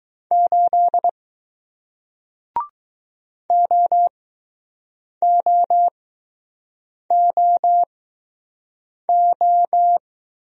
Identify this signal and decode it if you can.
Morse code, heard as a keyed tone, about 20 wpm, spaced as for 6 wpm, 700 Hz. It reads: 8 OOOO